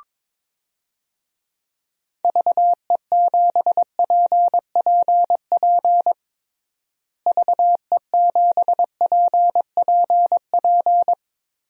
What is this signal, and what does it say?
Morse code, heard as a keyed tone, about 22 wpm, 700 Hz. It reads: VE7PPP VE7PPP
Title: DXCC-100 Call Signs - 6 Characters - Repeated 3x - 22wpm